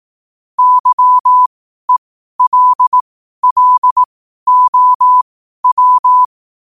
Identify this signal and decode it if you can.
Morse code, heard as a keyed tone, about 18 wpm, spaced as for 12 wpm, 1000 Hz. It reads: YELLOW
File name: qotdfiles / 12wpm